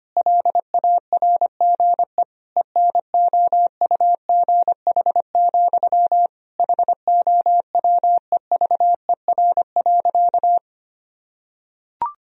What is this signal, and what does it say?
Morse code, heard as a keyed tone, about 25 wpm, 700 Hz. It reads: LARGE ENOUGH, HOWEVER.
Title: Sentences from Top 1000 Words - Review Part 1 - 25wpm